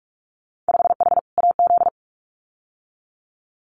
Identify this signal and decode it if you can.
Morse code, heard as a keyed tone, about 46 wpm, 700 Hz. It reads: 5H AZ
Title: Winter Field Day Exchanges Encore 45wpm